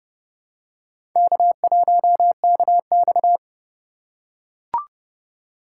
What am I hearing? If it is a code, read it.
Morse code, heard as a keyed tone, about 30 wpm, 700 Hz. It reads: K1KX
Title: USA Call Signs - 4 Characters - 2x Character Spacing - 30wpm